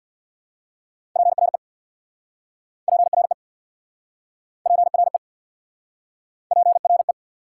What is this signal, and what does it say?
Morse code, heard as a keyed tone, about 69 wpm, 700 Hz. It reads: PRE PRE PRE PRE